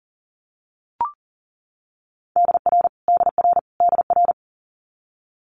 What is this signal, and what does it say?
Morse code, heard as a keyed tone, about 42 wpm, 700 Hz. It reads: DR DR DR
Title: Multi-letter Phonemes Repeated 3x 40wpm